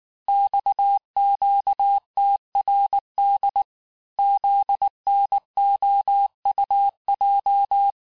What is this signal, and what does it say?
Morse code, heard as a keyed tone, about 19 wpm, 800 Hz. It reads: XQTRD ZNOUJ
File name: cours-du-lundi files / cours-1589-texte-chiffre-vitesse-18mpm